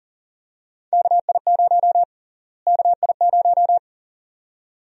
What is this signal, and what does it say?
Morse code, heard as a keyed tone, about 40 wpm, 700 Hz. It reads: KI0 KI0
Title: USA Call Sign Prefixes - Repeated 2x - 40wpm